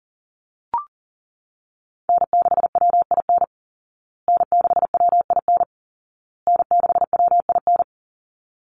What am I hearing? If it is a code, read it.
Morse code, heard as a keyed tone, about 40 wpm, 700 Hz. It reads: N6WIN N6WIN N6WIN
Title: Call Signs Repeated 2x - 40wpm